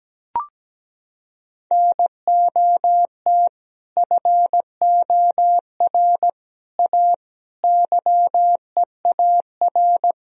Morse code, heard as a keyed tone, about 17 wpm, 700 Hz. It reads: NOT FOR A YEAR
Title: Sentences from Top 1000 Words - Review Part 1 - 17wpm